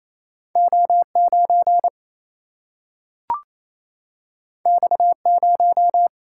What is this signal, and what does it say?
Morse code, heard as a keyed tone, about 28 wpm, 700 Hz. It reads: O9 X0
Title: Single-Letter Single-Number 28wpm